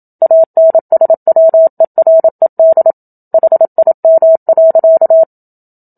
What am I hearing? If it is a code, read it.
Morse code, heard as a keyed tone, about 27 wpm, 650 Hz. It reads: ANSWERED HIM.
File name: Book Two / 01-Chapter 1 - 27 WPM